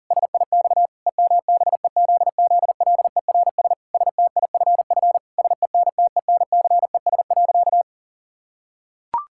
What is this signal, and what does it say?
Morse code, heard as a keyed tone, about 37 wpm, 700 Hz. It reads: SIX EMBEZZLERS STIFF SENTENCES.
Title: Pangram Challenge v2 - 40wpm